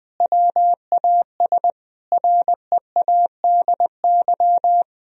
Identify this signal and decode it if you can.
Morse code, heard as a keyed tone, about 21 wpm, 700 Hz. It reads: WAS READY
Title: Sentences from Top 1000 Words - Review Part 2 - 20wpm